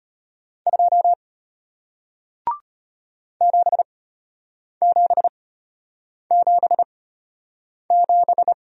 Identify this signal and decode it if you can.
Morse code, heard as a keyed tone, about 33 wpm, spaced as for 8 wpm, 700 Hz. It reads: 2 7777